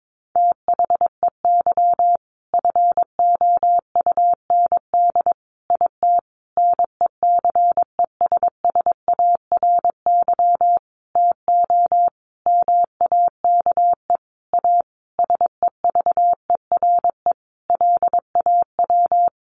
Morse code, heard as a keyed tone, about 22 wpm, 700 Hz. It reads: THEY FOUND IT NECESSARY TO MAKE A SEVERE LAW
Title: Sentences from Top 1400 Words 22wpm